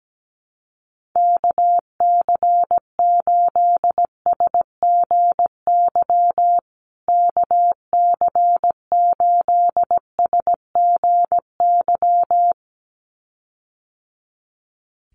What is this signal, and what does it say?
Morse code, heard as a keyed tone, about 17 wpm, 700 Hz. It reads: KC8SGY KC8SGY
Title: USA Call Signs Repeated 2x 17wpm